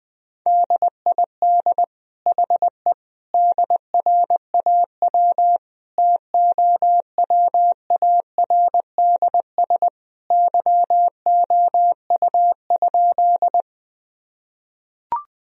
Morse code, heard as a keyed tone, about 21 wpm, 700 Hz. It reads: DID HE DRAW TOWARDS YOU?
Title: Sentences from Top 1000 Words - Review Part 2 - 20wpm